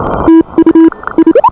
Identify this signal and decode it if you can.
Morse code, heard as a keyed tone, about 29 wpm, spaced as for 26 wpm, 310 Hz. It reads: TU I